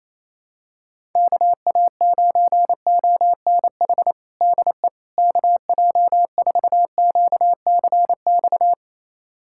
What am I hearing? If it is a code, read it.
Morse code, heard as a keyed tone, about 28 wpm, 700 Hz. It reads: KA9ONH DE KJ4QCX
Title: USA Callsign DE Callsign - 6 Characters - 28wpm